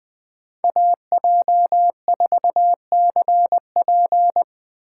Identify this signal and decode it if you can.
Morse code, heard as a keyed tone, about 21 wpm, 700 Hz. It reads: AJ4CP
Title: USA Call Signs - 3x Character Spacing - 20wpm